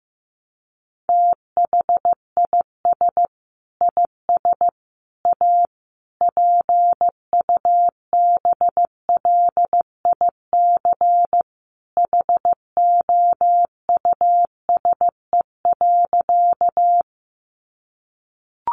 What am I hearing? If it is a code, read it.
Morse code, heard as a keyed tone, about 15 wpm, 700 Hz. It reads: THIS IS A PUBLIC HOUSE.
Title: Sentences from Top 600 Words 15wpm